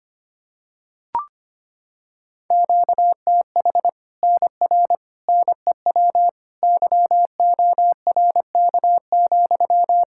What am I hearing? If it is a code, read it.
Morse code, heard as a keyed tone, about 25 wpm, 700 Hz. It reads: QTH NR NEW YORK,
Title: Practice QSO - Exchange 1 - Beginning - 25wpm